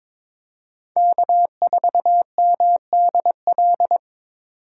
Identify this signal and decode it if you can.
Morse code, heard as a keyed tone, about 26 wpm, 700 Hz. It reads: K4MDL
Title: Speed Racing - USA 1x3 Call Signs 22wpm